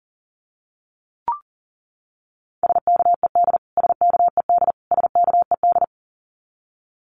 Morse code, heard as a keyed tone, about 40 wpm, 700 Hz. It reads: SKED SKED SKED
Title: Abbreviations Repeated 3x - 40wpm